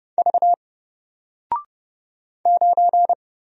Morse code, heard as a keyed tone, about 31 wpm, 700 Hz. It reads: V 9
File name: Single-Letter-Number-30wpm